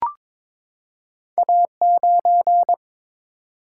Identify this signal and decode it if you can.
Morse code, heard as a keyed tone, about 22 wpm, 700 Hz. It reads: A9
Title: Single-Letter Single-Number 22wpm